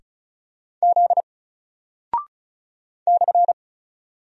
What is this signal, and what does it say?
Morse code, heard as a keyed tone, about 36 wpm, 700 Hz. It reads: Z /
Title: Single Character (Letter:Number:Punctuation:Prosign) Alt 35wpm